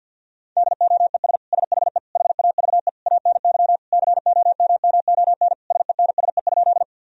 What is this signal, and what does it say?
Morse code, heard as a keyed tone, about 50 wpm, 700 Hz. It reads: DOES SHE HAVE ANY COMMON SENSE?
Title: Sentences from Top 1000 Words - Review Part 4 - 50wpm